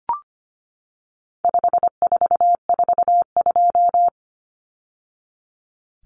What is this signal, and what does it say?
Morse code, heard as a keyed tone, about 25 wpm, 700 Hz. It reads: 5442
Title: Numbers - 4-Digit 25wpm